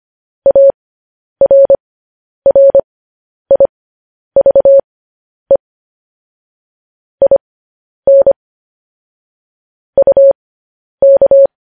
Morse code, heard as a keyed tone, about 25 wpm, spaced as for 10 wpm, 550 Hz. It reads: ARRIVE IN UK
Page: news-headlines-in-morse-code-at-10-wpm